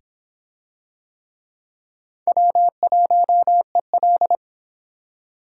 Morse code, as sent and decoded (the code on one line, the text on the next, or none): .-- .---- . .-..
W1EL